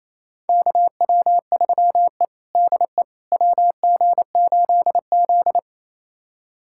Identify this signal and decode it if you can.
Morse code, heard as a keyed tone, about 28 wpm, 700 Hz. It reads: KW3E DE WG8Z
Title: USA Callsign DE Callsign - 4 Characters - 28wpm